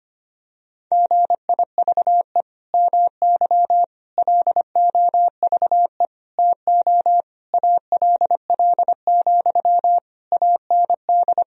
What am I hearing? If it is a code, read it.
Morse code, heard as a keyed tone, about 25 wpm, 700 Hz. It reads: GIVE MY LOVE TO ALL, AND
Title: Sentences from Top 1000 Words - Review Part 4 - 25wpm